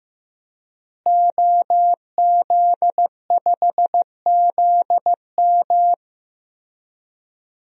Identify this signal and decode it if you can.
Morse code, heard as a keyed tone, about 15 wpm, 700 Hz. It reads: OZ5ZM
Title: DXCC-100 Call Signs - 5 Characters - Repeated 3x - 15wpm